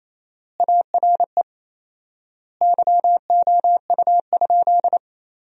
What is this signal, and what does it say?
Morse code, heard as a keyed tone, about 28 wpm, 700 Hz. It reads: ARE YOU?
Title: 4x Word-Spacing - Sentences from Top 100 Words - 28wpm